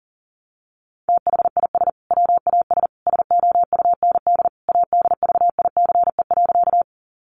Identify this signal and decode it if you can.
Morse code, heard as a keyed tone, about 40 wpm, 700 Hz. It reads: THIS WAS SOUND ADVICE.
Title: Sentences from Top 1100 Words 40wpm